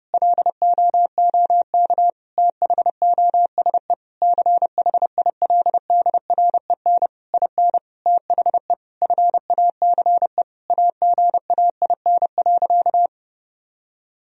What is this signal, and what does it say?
Morse code, heard as a keyed tone, about 30 wpm, 700 Hz. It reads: LOOK THOSE CHILDREN IN THE FACE AGAIN.
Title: Sentences from Top 1000 Words - Review Part 2 - 30wpm